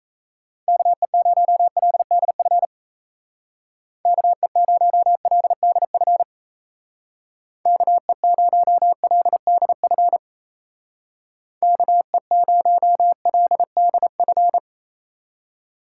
Morse code, heard as a keyed tone, about 34 wpm, 700 Hz. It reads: KE0LDF KE0LDF KE0LDF KE0LDF
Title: Speed Racing - USA 2x3 Call Signs 28wpm